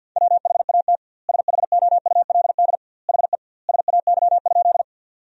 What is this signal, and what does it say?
Morse code, heard as a keyed tone, about 50 wpm, 700 Hz. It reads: WHAT SHOULD HE SAY?
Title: 2x Word-Spacing - Sentences from Top 300 Words - 50wpm